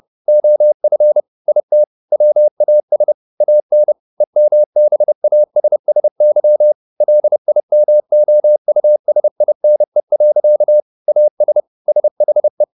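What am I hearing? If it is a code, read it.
Morse code, heard as a keyed tone, about 30 wpm, 600 Hz. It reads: OF IT WAS AN EMBASSY LIMOUSINE. AS SHE
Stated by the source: Morse code at 30 wpm, the novel Never by Ken Follett in English, from CW med Gnister.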